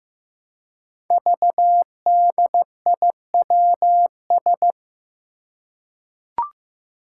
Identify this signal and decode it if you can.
Morse code, heard as a keyed tone, about 15 wpm, 700 Hz. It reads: VDIWS
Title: Five Challenging Letter-Number Combinations 15wpm